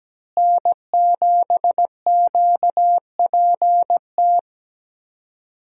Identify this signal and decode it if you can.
Morse code, heard as a keyed tone, about 17 wpm, 700 Hz. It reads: N7QPT
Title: USA Call Signs - 5 Characters - Repeated 3x - 17wpm